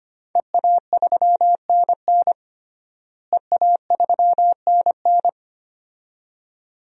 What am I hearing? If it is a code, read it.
Morse code, heard as a keyed tone, about 25 wpm, 700 Hz. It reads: EA3NN EA3NN
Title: DXCC-100 Call Signs - 5 Characters - Repeated 2x - 25wpm